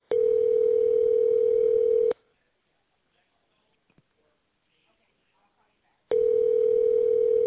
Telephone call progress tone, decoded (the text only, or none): ringback tone